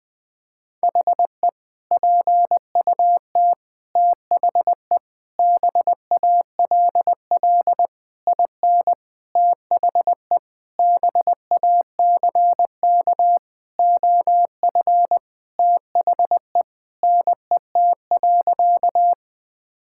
Morse code, heard as a keyed tone, about 20 wpm, 700 Hz. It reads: HE PUT THE BALL IN THE BACK OF THE NET.
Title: Sentences with 8-Word Phrases - 20wpm